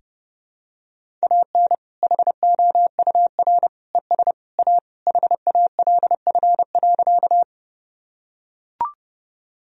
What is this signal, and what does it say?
Morse code, heard as a keyed tone, about 30 wpm, 700 Hz. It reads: AN HOUR ES A HALF.
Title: Sentences from Top 400 Words and Abbreviations 30wpm